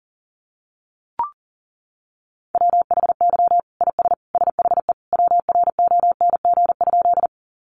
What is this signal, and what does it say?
Morse code, heard as a keyed tone, about 40 wpm, 700 Hz. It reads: WHY IS SHE WRONG?
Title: Sentences from Top 900 Words 40wpm